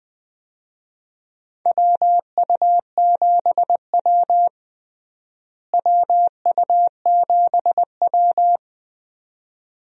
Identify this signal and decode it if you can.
Morse code, heard as a keyed tone, about 21 wpm, 700 Hz. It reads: WU7W WU7W
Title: USA Call Signs - 4 Characters - Repeated 2x - 20wpm